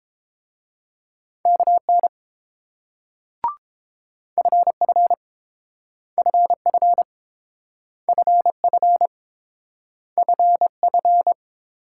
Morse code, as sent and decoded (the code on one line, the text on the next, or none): -.- -. / ..-. ..-. / ..-. ..-. / ..-. ..-. / ..-. ..-.
KN FF FF FF FF